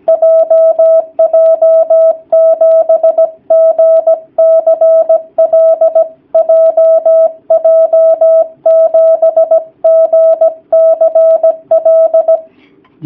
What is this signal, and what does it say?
Morse code, heard as a keyed tone, about 17 wpm, 650 Hz. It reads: JJ7GCL JJ7GCL
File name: jh8cbh / 2019cw / sp16- w